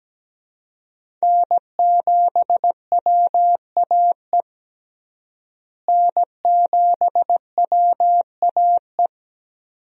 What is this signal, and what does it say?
Morse code, heard as a keyed tone, about 17 wpm, 700 Hz. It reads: N7WAE N7WAE
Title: USA Call Signs - 5 Characters - Repeated 2x - 17wpm